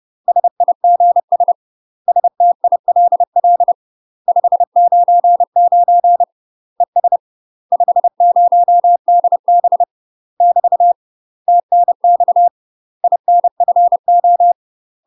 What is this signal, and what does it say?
Morse code, heard as a keyed tone, about 30 wpm, 700 Hz. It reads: SIGS STILL 599 ES 50DB <BT> TNX INFO
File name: UK MASTER QSOs TRAINING / 30WPM / UK MASTER QSOs 0000